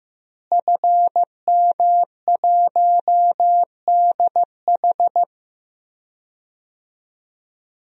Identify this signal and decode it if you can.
Morse code, heard as a keyed tone, about 15 wpm, 700 Hz. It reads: FM1DH
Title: DXCC-100 Call Signs - 5 Characters - Repeated 2x - 15wpm